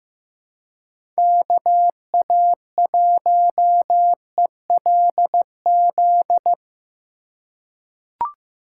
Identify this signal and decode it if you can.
Morse code, heard as a keyed tone, about 15 wpm, 700 Hz. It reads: KA1ELZ
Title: USA Call Signs - 6 Characters - 3x Character Spacing - 15wpm